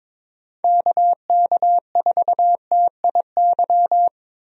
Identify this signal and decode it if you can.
Morse code, heard as a keyed tone, about 22 wpm, 700 Hz. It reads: KK4TIY
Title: USA Call Signs - 6 Characters - 22wpm